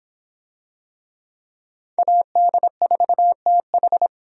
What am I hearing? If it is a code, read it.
Morse code, heard as a keyed tone, about 25 wpm, 700 Hz. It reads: AD4TH